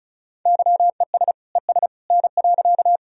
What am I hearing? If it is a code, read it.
Morse code, heard as a keyed tone, about 34 wpm, 700 Hz. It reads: YES ES N.
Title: Sentences from Top 500 Words and Abbreviations 35wpm